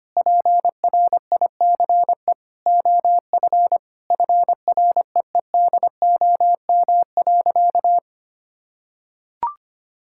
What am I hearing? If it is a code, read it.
Morse code, heard as a keyed tone, about 25 wpm, 700 Hz. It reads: PRICE OF FREEDOM.
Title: Sentences from Top 1000 Words - Review Part 5 - 25wpm